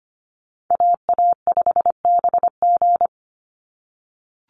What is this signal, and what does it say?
Morse code, heard as a keyed tone, about 25 wpm, 700 Hz. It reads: AA5BG
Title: Call Signs - 25wpm